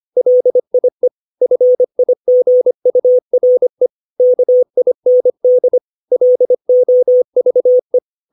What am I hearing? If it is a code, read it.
Morse code, heard as a keyed tone, about 25 wpm, 500 Hz. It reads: LIE FIGURE KIND LOVE
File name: 632 words / 632 Words 25WPM